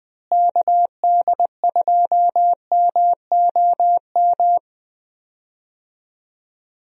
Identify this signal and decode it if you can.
Morse code, heard as a keyed tone, about 20 wpm, 700 Hz. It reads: KD2MOM
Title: USA Call Signs - 6 Characters - Repeated 3x - 20wpm